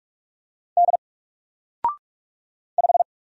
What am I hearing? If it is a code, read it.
Morse code, heard as a keyed tone, about 44 wpm, 700 Hz. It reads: D 5
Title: Single Letter-Number 45wpm